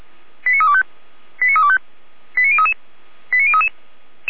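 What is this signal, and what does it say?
Five-tone selective call (ZVEI), heard as quick runs of five tones, tones of about 70 ms. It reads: 89325 89325 89033 89033
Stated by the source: ZVEI five-tone selective call 89340, received by SDR.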